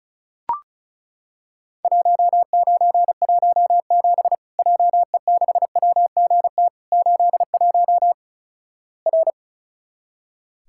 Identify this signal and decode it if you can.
Morse code, heard as a keyed tone, about 34 wpm, 700 Hz. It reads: 1917 JE6WGT 81 R
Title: Worked All Europe DX Contest - QTC Exchange - 35wpm